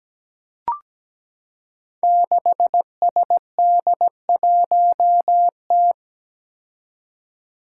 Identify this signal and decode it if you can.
Morse code, heard as a keyed tone, about 17 wpm, 700 Hz. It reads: 6SD1T
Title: Five Challenging Letter-Number Combinations 17wpm